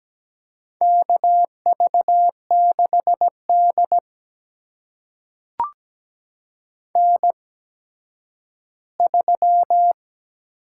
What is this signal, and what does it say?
Morse code, heard as a keyed tone, about 17 wpm, spaced as for 10 wpm, 700 Hz. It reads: KV6D N3
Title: USA Call Signs - 4 Characters - 8x Character Spacing - 17wpm